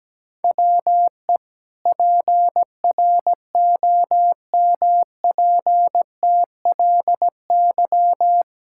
Morse code, heard as a keyed tone, about 17 wpm, 700 Hz. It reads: WE PROMPTLY
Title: Pangram Challenge v2 - 17wpm